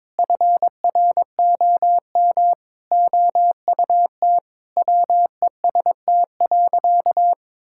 Morse code, heard as a keyed tone, about 22 wpm, 700 Hz. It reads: FROM OUT WEST.
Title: Sentences from Top 1000 Words - Review Part 5 - 22wpm